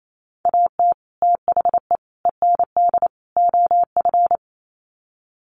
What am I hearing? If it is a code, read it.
Morse code, heard as a keyed tone, about 28 wpm, 700 Hz. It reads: AT THE END OF
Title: Sets of 4 Words 28wpm